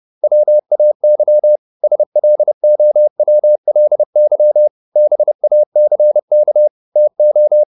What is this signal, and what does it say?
Morse code, heard as a keyed tone, about 30 wpm, 600 Hz. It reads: WAY SLOWLY BACK TO